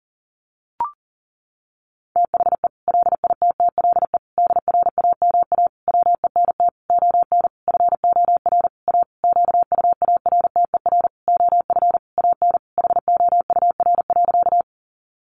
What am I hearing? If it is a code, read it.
Morse code, heard as a keyed tone, about 40 wpm, 700 Hz. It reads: THE LITTLE DRAMA WENT ON FOR A QUARTER OF AN HOUR.
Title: Sentences from Top 1500 Words 40wpm